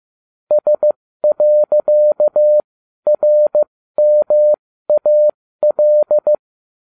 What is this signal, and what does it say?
Morse code, heard as a keyed tone, about 15 wpm, spaced as for 12 wpm, 600 Hz. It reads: S.RMAL
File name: cw / ubacw12 2